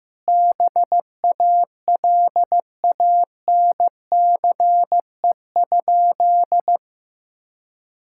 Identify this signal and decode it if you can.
Morse code, heard as a keyed tone, about 16 wpm, 700 Hz. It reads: BALANCE?
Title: Sentences from Top 1000 Words - Review Part 6 - 15wpm